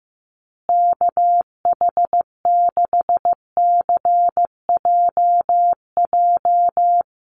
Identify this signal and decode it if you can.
Morse code, heard as a keyed tone, about 15 wpm, 700 Hz. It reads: KH6CJJ
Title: Call Signs Repeated 2x - 15wpm